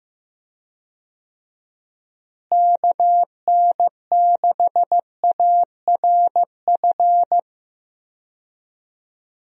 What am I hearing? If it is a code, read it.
Morse code, heard as a keyed tone, about 15 wpm, 700 Hz. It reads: KN6ARF
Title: USA Call Signs - 6 Characters - Repeated 3x - 15wpm